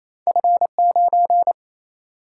Morse code, heard as a keyed tone, about 28 wpm, 700 Hz. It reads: F 9
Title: Single-Letter Single-Number 28wpm